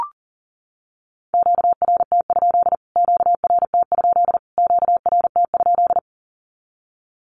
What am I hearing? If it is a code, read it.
Morse code, heard as a keyed tone, about 40 wpm, 700 Hz. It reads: QRT? QRT? QRT?
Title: Q-Codes Repeated 3x 40wpm